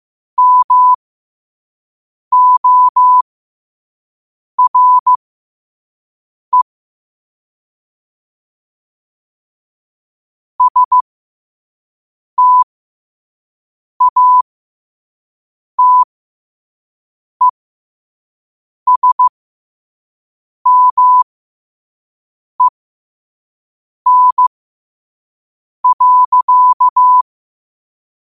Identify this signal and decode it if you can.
Morse code, heard as a keyed tone, about 15 wpm, spaced as for 5 wpm, 1000 Hz. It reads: MORE STATESMEN.
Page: qotd-at-5-wpm